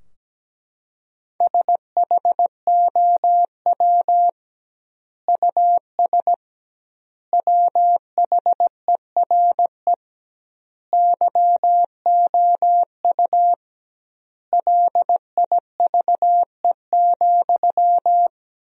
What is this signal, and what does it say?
Morse code, heard as a keyed tone, about 17 wpm, 700 Hz. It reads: SHOW US WHERE YOU LIVE,
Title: 2x Word-Spacing - Sentences from Top 200 Words - 17wpm